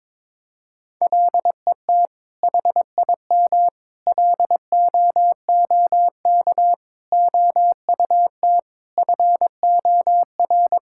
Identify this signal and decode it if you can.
Morse code, heard as a keyed tone, about 22 wpm, 700 Hz. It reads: LET HIM LOOK OUT FOR
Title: Sentences from Top 1600 Words 22wpm